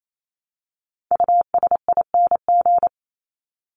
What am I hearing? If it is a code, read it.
Morse code, heard as a keyed tone, about 28 wpm, 700 Hz. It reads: USING
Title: ING Suffix - Top 100 Words with Base Words 28wpm